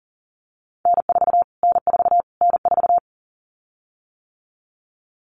Morse code, heard as a keyed tone, about 40 wpm, 700 Hz. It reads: N4 N4 N4
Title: Callsign Prefixes for Top 25 DXCC - Repeated 3x - 40wpm